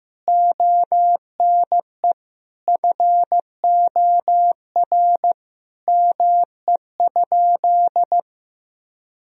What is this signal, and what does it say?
Morse code, heard as a keyed tone, about 15 wpm, 700 Hz. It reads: ONE FOR ME?
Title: Sentences from Top 1000 Words - Review Part 3 - 15wpm